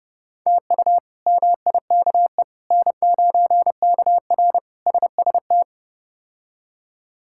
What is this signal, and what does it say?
Morse code, heard as a keyed tone, about 29 wpm, 700 Hz. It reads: TU MIKE N9KR SST
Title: K1USN SST QSO Practice - 30wpm